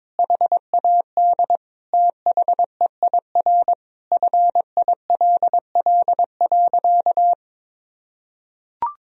Morse code, heard as a keyed tone, about 22 wpm, 690 Hz. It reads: HAD THEIR FILL.
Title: Sentences from Top 1000 Words - Review Part 3 - 22wpm